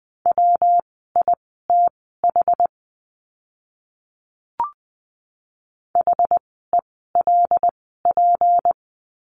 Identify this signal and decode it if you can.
Morse code, heard as a keyed tone, about 20 wpm, spaced as for 15 wpm, 700 Hz. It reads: WITH HELP
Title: 192 Lesson 31f - Letter v - Words 20wpm